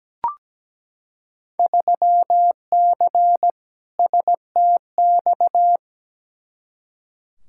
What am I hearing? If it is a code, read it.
Morse code, heard as a keyed tone, about 17 wpm, 700 Hz. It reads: 3C STX
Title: Field Day Exchanges Encore - 17wpm